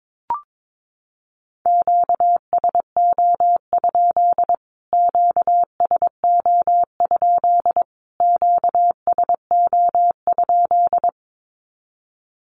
Morse code, heard as a keyed tone, about 22 wpm, 700 Hz. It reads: QSO? QSO? QSO?
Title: Q-Codes Repeated 3x 22wpm